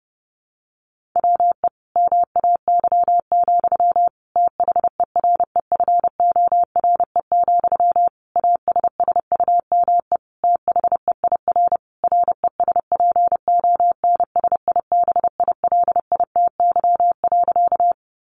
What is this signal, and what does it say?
Morse code, heard as a keyed tone, about 30 wpm, 700 Hz. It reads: WE MAY, THEREFORE, ASSUME THEIR RESPONSIBILITY.
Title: Sentences from Top 1500 Words 30wpm